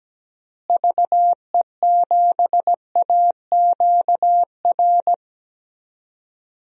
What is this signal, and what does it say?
Morse code, heard as a keyed tone, about 17 wpm, 700 Hz. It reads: VE7AQR
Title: DXCC-100 Call Signs - 6 Characters - Repeated 2x - 17wpm